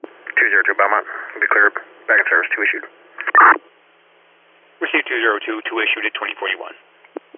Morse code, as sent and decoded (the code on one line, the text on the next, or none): none